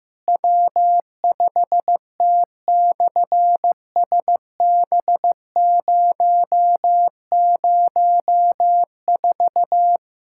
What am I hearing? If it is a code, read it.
Morse code, heard as a keyed tone, about 15 wpm, 700 Hz. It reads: W5T/SB004
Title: SOTA QSO Practice 15wpm